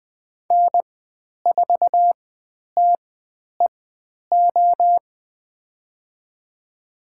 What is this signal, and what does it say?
Morse code, heard as a keyed tone, about 20 wpm, spaced as for 10 wpm, 700 Hz. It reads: N4TEO
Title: 102 Lesson 17g - Letter u - Call Signs 20wpm